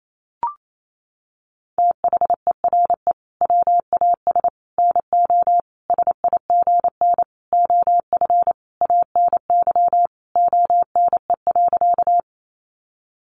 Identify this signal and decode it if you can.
Morse code, heard as a keyed tone, about 28 wpm, 700 Hz. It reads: THERE WAS NO SIGN OF ANY ONE.
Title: Sentences from Top 1000 Words 28wpm